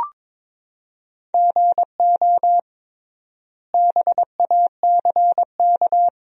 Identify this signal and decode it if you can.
Morse code, heard as a keyed tone, about 22 wpm, 700 Hz. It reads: GO BACK
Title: 3x Word-Spacing - Sentences from Top 300 Words - 22wpm